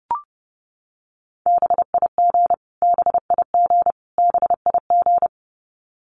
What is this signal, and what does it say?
Morse code, heard as a keyed tone, about 30 wpm, 700 Hz. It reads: BIG BIG BIG
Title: Top 500 Words Repeated 30wpm